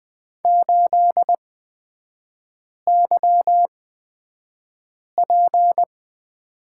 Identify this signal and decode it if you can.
Morse code, heard as a keyed tone, about 20 wpm, spaced as for 5 wpm, 690 Hz. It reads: 8YP
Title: 229 Lesson 37e - Letter x - Three Characters 20wpm